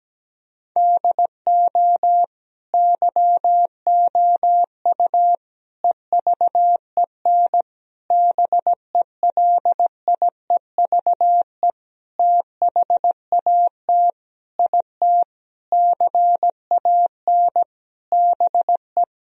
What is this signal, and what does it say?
Morse code, heard as a keyed tone, about 17 wpm, 700 Hz. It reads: DO YOU EVEN BELIEVE THAT IT CAN BE
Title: Sentences from Top 1000 Words - Review Part 1 - 17wpm